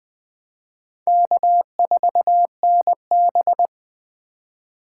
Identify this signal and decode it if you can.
Morse code, heard as a keyed tone, about 20 wpm, 700 Hz. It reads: K4NB
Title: Speed Racing - USA 1x2 Call Signs 17wpm